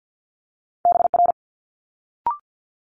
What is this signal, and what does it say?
Morse code, heard as a keyed tone, about 50 wpm, 700 Hz. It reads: BR
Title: Multi-letter Phonemes Quick Fire 50wpm